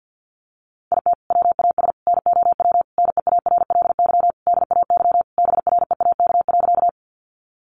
Morse code, heard as a keyed tone, about 50 wpm, 700 Hz. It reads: IT WAS NOW NEARLY DAY BREAK.
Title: Sentences from Top 1300 Words 50wpm